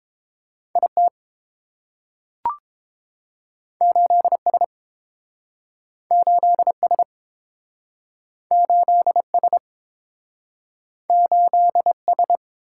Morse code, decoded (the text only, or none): I T 8 S 8 S 8 S 8 S